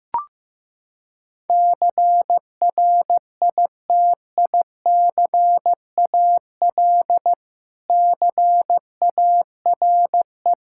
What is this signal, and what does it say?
Morse code, heard as a keyed tone, about 15 wpm, 700 Hz. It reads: CRITICAL CARE
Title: K1USN SST QSO Practice 15wpm